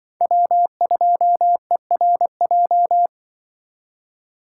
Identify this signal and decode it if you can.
Morse code, heard as a keyed tone, about 24 wpm, 700 Hz. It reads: W2ERJ
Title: Speed Racing - USA 1x3 Call Signs 20wpm